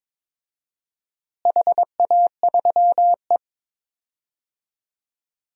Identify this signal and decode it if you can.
Morse code, heard as a keyed tone, about 22 wpm, 700 Hz. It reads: HA3E
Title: DXCC-100 Call Signs - 4 Characters - Repeated 2x - 22wpm